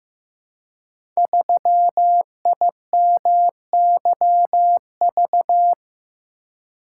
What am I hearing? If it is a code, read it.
Morse code, heard as a keyed tone, about 15 wpm, 700 Hz. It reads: 3IMYV
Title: Code Groups - 5 Letter:Numbers - 15wpm